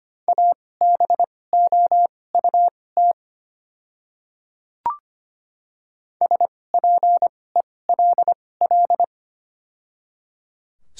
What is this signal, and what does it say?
Morse code, heard as a keyed tone, about 25 wpm, spaced as for 18 wpm, 700 Hz. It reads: ABOUT SPELL
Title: 2x Character Spacing - Top 200 Words - 25wpm